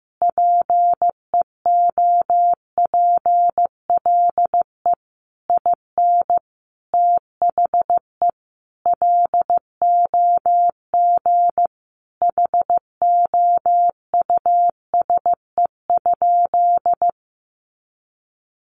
Morse code, heard as a keyed tone, about 15 wpm, 700 Hz. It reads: PEOPLE IN THE LOG HOUSE?
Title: Sentences from Top 1400 Words 15wpm